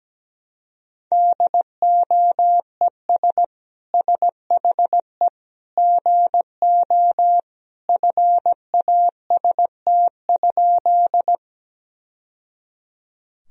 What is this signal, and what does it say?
Morse code, heard as a keyed tone, about 17 wpm, 700 Hz. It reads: DOES SHE GO FAST?
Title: Sentences from Top 1000 Words - Review Part 2 - 17wpm